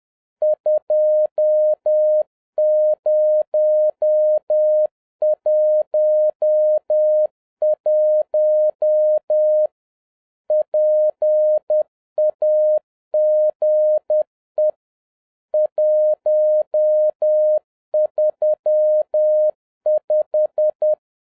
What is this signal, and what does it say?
Morse code, heard as a keyed tone, about 10 wpm, 610 Hz. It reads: 2011 PAGE 135
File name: 2011-07-mp3 / n21 GREETINGS FROM SINGAPORE 10wpm